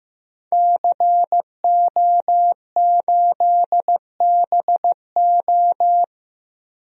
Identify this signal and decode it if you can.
Morse code, heard as a keyed tone, about 15 wpm, 700 Hz. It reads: CO8BO
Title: DXCC-100 Call Signs - 5 Characters - 15wpm